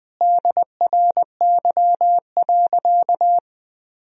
Morse code, heard as a keyed tone, about 20 wpm, 700 Hz. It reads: DRY.
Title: Sentences from Top 1000 Words - Review Part 3 - 20wpm